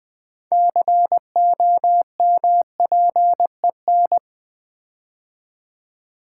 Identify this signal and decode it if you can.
Morse code, heard as a keyed tone, about 20 wpm, 700 Hz. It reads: COMPEN
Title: Build A Word - 6 Letters At A Time - 12-Letter Words - 20wpm